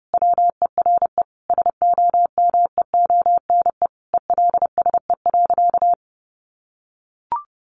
Morse code, heard as a keyed tone, about 30 wpm, 700 Hz. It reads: WERE SOMEONE ELSE.
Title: Sentences from Top 800 Words 30wpm